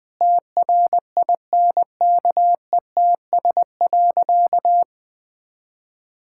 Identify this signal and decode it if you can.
Morse code, heard as a keyed tone, about 20 wpm, 700 Hz. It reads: TRINKETS.
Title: Sentences with 2-Word Phrases - 20wpm